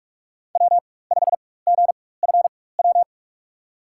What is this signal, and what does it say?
Morse code, heard as a keyed tone, about 45 wpm, spaced as for 21 wpm, 700 Hz. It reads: W5GFW
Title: USA Call Signs - 5 Characters - 4x Character Spacing - 45wpm